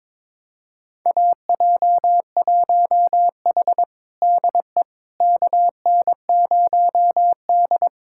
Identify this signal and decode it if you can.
Morse code, heard as a keyed tone, about 22 wpm, 700 Hz. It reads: AJ1H DE KN0D
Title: USA Callsign DE Callsign - 4 Characters - 22wpm